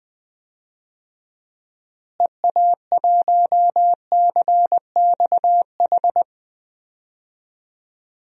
Morse code, heard as a keyed tone, about 20 wpm, 700 Hz. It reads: EA1CXH
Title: DXCC-100 Call Signs - 6 Characters - Repeated 3x - 20wpm